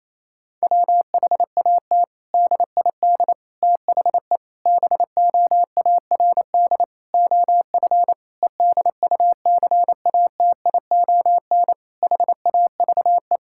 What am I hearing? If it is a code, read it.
Morse code, heard as a keyed tone, about 28 wpm, 700 Hz. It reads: WHAT DID THE BOARD OF EDUCATION HAVE
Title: Sentences from Top 1000 Words - Review Part 6 - 28wpm